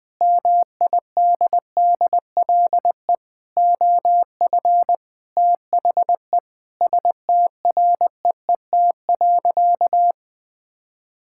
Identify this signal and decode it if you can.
Morse code, heard as a keyed tone, about 20 wpm, 700 Hz. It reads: MIDDLE OF THE STREET.
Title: Sentences with 4-Word Phrases - Spoken Beginning - 20wpm